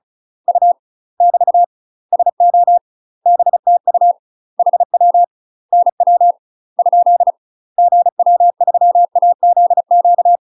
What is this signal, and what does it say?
Morse code, heard as a keyed tone, about 36 wpm, 700 Hz. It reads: U <BT> SO BTU HW NW ? GW3AZQ
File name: UK MASTER QSOs TRAINING / 35WPM / UK MASTER QSOs 0023